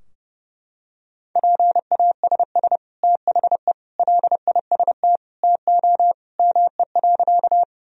Morse code, heard as a keyed tone, about 32 wpm, 700 Hz. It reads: PASS THE LIST TO ME.
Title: Sentences from Top 1000 Words - Review Part 2 - 30wpm